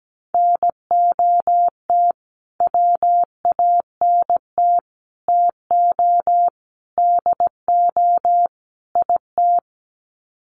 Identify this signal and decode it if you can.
Morse code, heard as a keyed tone, about 17 wpm, 700 Hz. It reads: NOT WANT TO DO IT
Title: Sets of 5 Words Encore 17wpm